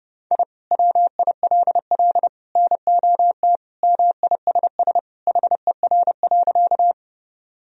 Morse code, heard as a keyed tone, about 30 wpm, 700 Hz. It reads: I WILL NOT MISS HER.
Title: Sentences from Top 1000 Words - Review Part 3 - 30wpm